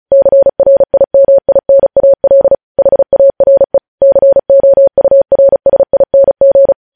Morse code, heard as a keyed tone, about 36 wpm, 550 Hz. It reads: CRIMINAL HARE COURSING
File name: podcast / News headlines-1x-35x35